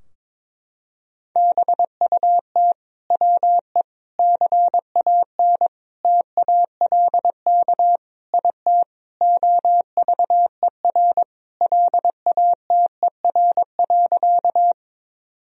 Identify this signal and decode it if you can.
Morse code, heard as a keyed tone, about 21 wpm, 700 Hz. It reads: BUT WE CAN TALK IT OVER LATER.
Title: Sentences from Top 1000 Words - Review Part 4 - 22wpm